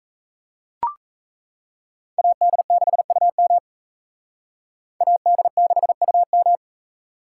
Morse code, heard as a keyed tone, about 35 wpm, 700 Hz. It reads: AD6UM AD6UM
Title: Speed Racing - USA 2x2 Call Signs 28wpm